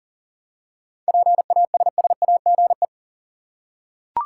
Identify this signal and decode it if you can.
Morse code, heard as a keyed tone, about 39 wpm, 700 Hz. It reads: PASSAGE
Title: Seven Letter Words - 40wpm